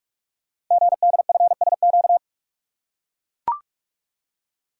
Morse code, heard as a keyed tone, about 44 wpm, 700 Hz. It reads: GDFSQ
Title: Code Groups - 5 Letters - 45wpm